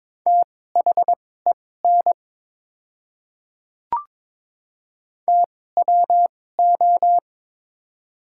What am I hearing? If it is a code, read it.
Morse code, heard as a keyed tone, about 22 wpm, spaced as for 16 wpm, 690 Hz. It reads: THEN TWO
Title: 2x Character Spacing - Top 100 Words - 22wpm